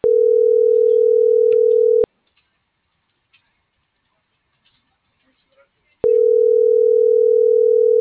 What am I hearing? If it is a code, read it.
Ringback tone (North America), 440 + 480 Hz, with speech.
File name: CENTRAL TERRESTRE / out els..laz.elo.mo.24